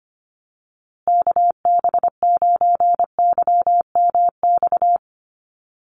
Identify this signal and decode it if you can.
Morse code, heard as a keyed tone, about 25 wpm, 700 Hz. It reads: KB9YMX
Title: USA Call Signs Repeated 3x 25wpm